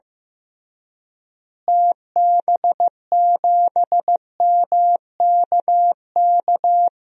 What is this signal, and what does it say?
Morse code, heard as a keyed tone, about 15 wpm, 700 Hz. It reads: TB7MKK
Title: DXCC-100 Call Signs - 6 Characters - Repeated 2x - 15wpm